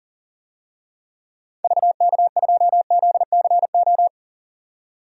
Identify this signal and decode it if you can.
Morse code, heard as a keyed tone, about 40 wpm, 700 Hz. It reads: VK2ZCO
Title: DXCC-100 Call Signs - 6 Characters - 40wpm